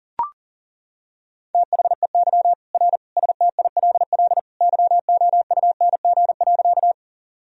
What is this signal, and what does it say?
Morse code, heard as a keyed tone, about 37 wpm, 700 Hz. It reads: THEY R STILL YOUNG.
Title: Sentences from Top 400 Words and Abbreviations 40wpm